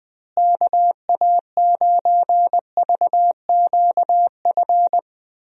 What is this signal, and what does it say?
Morse code, heard as a keyed tone, about 20 wpm, 700 Hz. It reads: KA9VQF
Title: USA Call Signs - 6 Characters - Repeated 2x - 20wpm